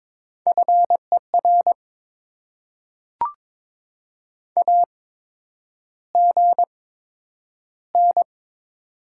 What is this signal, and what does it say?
Morse code, heard as a keyed tone, about 22 wpm, spaced as for 9 wpm, 700 Hz. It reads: FER AGN
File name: Top 20 QSO Elements - 8x Character Spacing - 22wpm